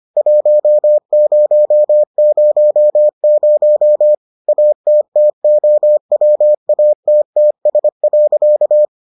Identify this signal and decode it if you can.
Morse code, heard as a keyed tone, about 25 wpm, 600 Hz. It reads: 1000 ATTOWATTS.